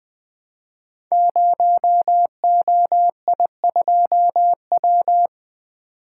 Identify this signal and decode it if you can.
Morse code, heard as a keyed tone, about 20 wpm, 700 Hz. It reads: 0OI2W
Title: Code Groups - 5 Letter:Numbers - 20wpm